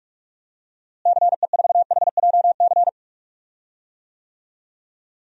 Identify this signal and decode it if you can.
Morse code, heard as a keyed tone, about 46 wpm, 700 Hz. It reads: CE4HJC
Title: DXCC-100 Call Signs - 6 Characters - Repeated 2x - 45wpm